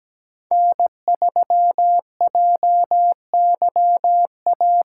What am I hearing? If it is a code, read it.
Morse code, heard as a keyed tone, about 17 wpm, 700 Hz. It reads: N3JYA
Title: USA Call Signs - 2x Character Spacing - 17wpm